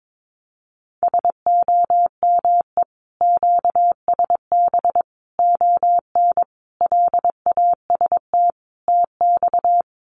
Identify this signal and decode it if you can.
Morse code, heard as a keyed tone, about 22 wpm, 700 Hz. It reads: SOME QSB ON LAST TX
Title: Common QSO Phrases 22wpm